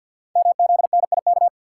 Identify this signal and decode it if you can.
Morse code, heard as a keyed tone, about 48 wpm, 700 Hz. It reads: MZNIK
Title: Code Groups - 5 Letters - 50wpm